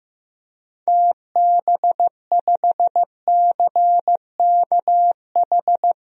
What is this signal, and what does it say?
Morse code, heard as a keyed tone, about 16 wpm, 700 Hz. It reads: TB5CKH
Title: DXCC-100 Call Signs - 6 Characters - 15wpm